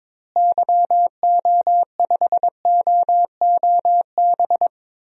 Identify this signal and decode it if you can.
Morse code, heard as a keyed tone, about 22 wpm, 700 Hz. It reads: YO5OOB
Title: DXCC-100 Call Signs - 6 Characters - Repeated 3x - 22wpm